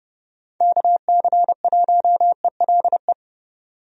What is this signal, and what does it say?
Morse code, heard as a keyed tone, about 31 wpm, 700 Hz. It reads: KC1ELE
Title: USA Call Signs - 6 Characters - 30wpm